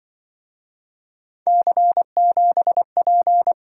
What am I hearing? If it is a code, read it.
Morse code, heard as a keyed tone, about 25 wpm, 700 Hz. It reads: C7P
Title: Speed-Racing - Three Letter-Number Combinations - 20wpm